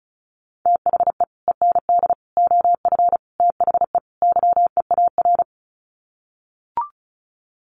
Morse code, heard as a keyed tone, about 34 wpm, 700 Hz. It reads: THE END OF THE YEAR
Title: Sets of 5 Words 35wpm